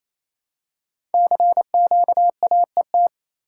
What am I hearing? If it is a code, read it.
Morse code, heard as a keyed tone, about 27 wpm, 700 Hz. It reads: CQAET
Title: Code Groups - 5 Letter:Numbers - 28wpm